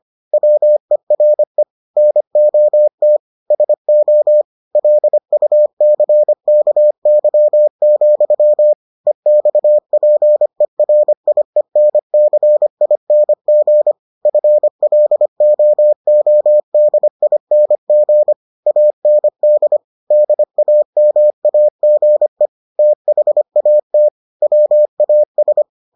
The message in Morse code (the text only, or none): WERE NOT SO LUCKY, EXPERIENCING FLOODING AND DAMAGE THAT WAS